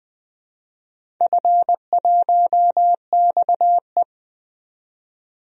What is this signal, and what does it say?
Morse code, heard as a keyed tone, about 20 wpm, 700 Hz. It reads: F1XE
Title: DXCC-100 Call Signs - 4 Characters - 20wpm